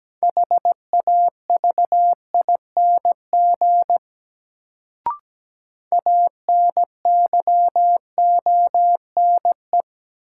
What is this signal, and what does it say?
Morse code, heard as a keyed tone, about 17 wpm, 700 Hz. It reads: HAVING ANYONE
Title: Sets of 5 Words Encore - New Word Drill - 17wpm